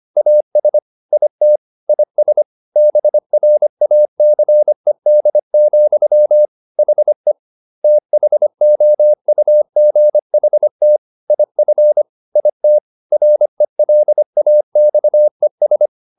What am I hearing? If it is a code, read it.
Morse code, heard as a keyed tone, about 25 wpm, 600 Hz. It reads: AS IT IS BRACED, HE THOUGHT IF IT RELAXES